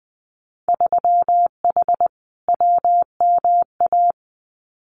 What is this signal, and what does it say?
Morse code, heard as a keyed tone, about 20 wpm, 700 Hz. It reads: 3H WMA
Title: Winter Field Day Exchanges Encore 20wpm